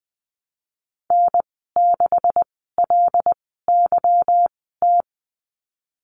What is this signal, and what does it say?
Morse code, heard as a keyed tone, about 19 wpm, spaced as for 14 wpm, 700 Hz. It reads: N6LYT
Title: 167 Lesson 27g - Number 7 - Call Signs 20wpm